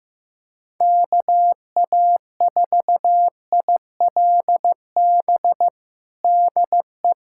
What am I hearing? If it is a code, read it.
Morse code, heard as a keyed tone, about 15 wpm, 700 Hz. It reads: KA4ILB DE
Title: USA Callsign DE Callsign 15wpm